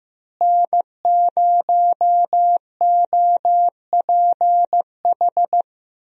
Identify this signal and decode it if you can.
Morse code, heard as a keyed tone, about 16 wpm, 700 Hz. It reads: N0OPH
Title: USA Call Signs - 5 Characters - Repeated 3x - 15wpm